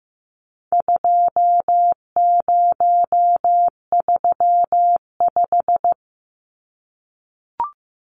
Morse code, heard as a keyed tone, about 15 wpm, 700 Hz. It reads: 2035
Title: Numbers - 4-Digit 15wpm